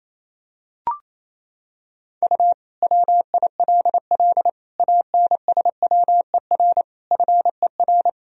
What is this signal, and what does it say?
Morse code, heard as a keyed tone, about 28 wpm, 700 Hz. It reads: U WILL ANSWER FER
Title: Sentences from Top 300 Words and Abbreviations 28wpm